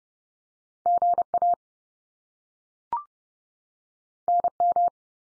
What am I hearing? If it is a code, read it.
Morse code, heard as a keyed tone, about 30 wpm, 700 Hz. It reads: GA NM
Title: US States 30WPM